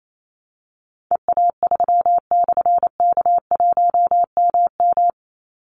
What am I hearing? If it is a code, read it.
Morse code, heard as a keyed tone, about 28 wpm, 700 Hz. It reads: EA3/K1MM
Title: Call Signs - 28wpm